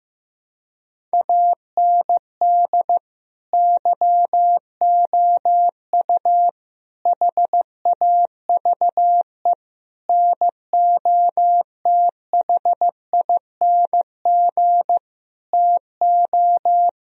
Morse code, heard as a keyed tone, about 15 wpm, 700 Hz. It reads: AND YOU HAVE NOTHING TO
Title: Sentences from Top 1000 Words - Review Part 1 - 15wpm